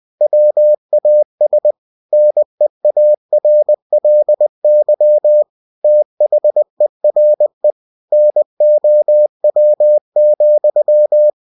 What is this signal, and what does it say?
Morse code, heard as a keyed tone, about 20 wpm, 600 Hz. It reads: WAS NEARLY THERE NOW,
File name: cw-books / winnie-the-pooh / winnie-20wpm-page-0003